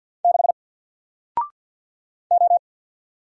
Morse code, heard as a keyed tone, about 48 wpm, 700 Hz. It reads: 6 O
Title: Single Letter-Number Alt 50wpm